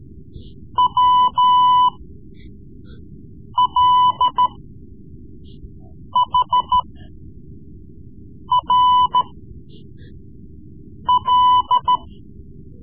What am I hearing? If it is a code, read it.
Morse code, heard as a keyed tone, about 13 wpm, spaced as for 4 wpm, 1000 Hz. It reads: WLHRL
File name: Morse Lessons / Morse Lessons / Tape-3B